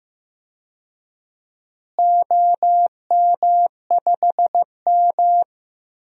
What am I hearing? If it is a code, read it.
Morse code, heard as a keyed tone, about 15 wpm, 700 Hz. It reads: OM5M
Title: DXCC-100 Call Signs - 4 Characters - Repeated 2x - 15wpm